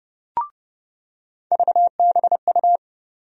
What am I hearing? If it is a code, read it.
Morse code, heard as a keyed tone, about 30 wpm, 700 Hz. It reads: VBU
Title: USA Call Sign Postfixes - 3 Letters 30wpm